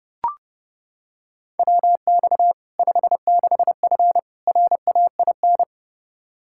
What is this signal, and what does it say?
Morse code, heard as a keyed tone, about 30 wpm, 700 Hz. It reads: WX 56F RAIN
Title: Practice QSO - Exchange 2 - Info - 30wpm